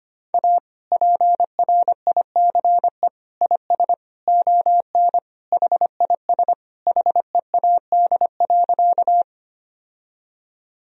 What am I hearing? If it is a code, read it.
Morse code, heard as a keyed tone, about 25 wpm, 700 Hz. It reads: A PRICE IS ON HIS HEAD.
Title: Sentences from Top 1000 Words - Review Part 4 - 25wpm